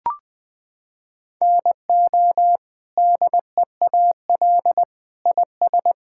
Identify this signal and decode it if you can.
Morse code, heard as a keyed tone, about 21 wpm, 700 Hz. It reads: NO DEAL IS
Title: Sets of 8 Words - 20wpm